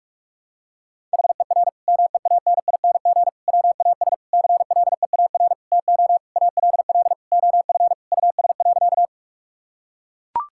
Morse code, heard as a keyed tone, about 41 wpm, 700 Hz. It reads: HER MEANING WAS CLEAR TO ALL OF US.
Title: Sentences from Top 1000 Words - Review Part 6 - 45wpm